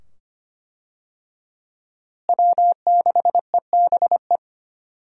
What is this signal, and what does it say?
Morse code, heard as a keyed tone, about 25 wpm, 700 Hz. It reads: W6EBE
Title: USA Call Signs - 5 Characters - Repeated 2x - 25wpm